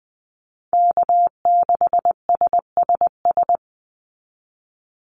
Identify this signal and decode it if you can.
Morse code, heard as a keyed tone, about 20 wpm, 700 Hz. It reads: K6SSS
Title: Call Signs - 20wpm